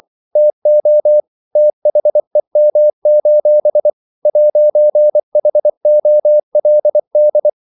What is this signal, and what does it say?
Morse code, heard as a keyed tone, about 24 wpm, 600 Hz. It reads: TO THEM: 'HOLD
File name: CW / Practice / If24